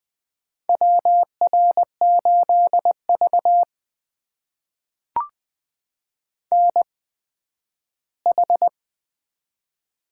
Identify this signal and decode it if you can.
Morse code, heard as a keyed tone, about 20 wpm, spaced as for 12 wpm, 700 Hz. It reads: WR8V NH
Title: USA Call Signs - 4 Characters - 8x Character Spacing - 20wpm